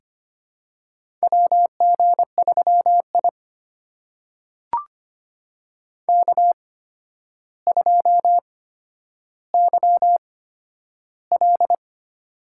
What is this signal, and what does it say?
Morse code, heard as a keyed tone, about 25 wpm, spaced as for 11 wpm, 700 Hz. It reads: WG3I K2YL
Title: USA Call Signs - 4 Characters - 8x Character Spacing - 25wpm